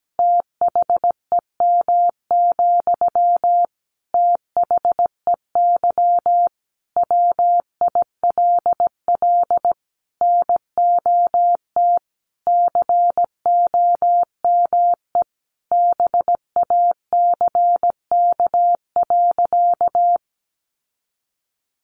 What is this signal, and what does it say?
Morse code, heard as a keyed tone, about 17 wpm, 700 Hz. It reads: THEM, THEY WILL NOT COME BACK.
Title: Sentences from Top 1300 Words 17wpm